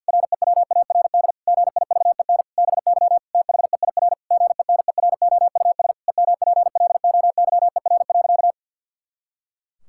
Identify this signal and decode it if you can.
Morse code, heard as a keyed tone, about 50 wpm, 700 Hz. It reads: REWARD GIVEN BY THEIR GENEROUS EMPLOYER.
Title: Pangram Challenge v2 - 50wpm